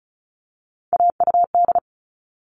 Morse code, heard as a keyed tone, about 34 wpm, 700 Hz. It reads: AUD
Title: All Cut Numbers - 3-Digit 35wpm